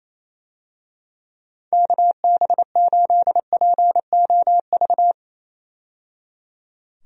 Morse code, heard as a keyed tone, about 27 wpm, 700 Hz. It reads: KB8POV
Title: USA Call Signs - 6 Characters - 28wpm